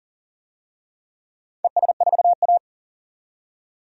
Morse code, heard as a keyed tone, about 40 wpm, 700 Hz. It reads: ES4A
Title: DXCC-100 Call Signs - 4 Characters - 40wpm